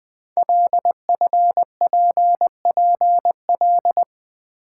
Morse code, heard as a keyed tone, about 21 wpm, 700 Hz. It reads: LFPPL
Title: Code Groups - 5 Letters - 20wpm